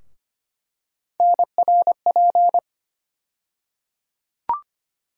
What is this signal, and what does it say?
Morse code, heard as a keyed tone, about 25 wpm, 700 Hz. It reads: NRP
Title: USA Call Sign Postfixes - 3 Letters 25wpm